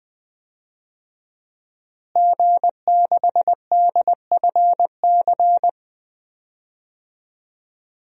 Morse code, heard as a keyed tone, about 21 wpm, 700 Hz. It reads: G6DFC
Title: DXCC-100 Call Signs - 5 Characters - Repeated 3x - 20wpm